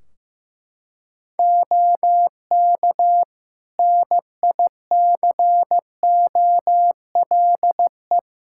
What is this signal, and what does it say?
Morse code, heard as a keyed tone, about 15 wpm, 700 Hz. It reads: OK NICOLE
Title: Practice QSO - Exchange 3 - More Info - 15wpm